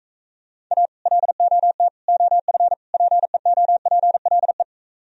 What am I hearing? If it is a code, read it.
Morse code, heard as a keyed tone, about 44 wpm, 700 Hz. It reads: A LOT OF PEOPLE